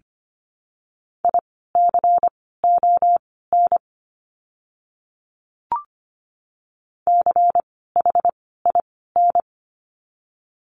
Morse code, heard as a keyed tone, about 25 wpm, spaced as for 16 wpm, 700 Hz. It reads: ICON CHIN
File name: course / 25:10wpm / 094 Lesson 16f - Letter c - Words 25wpm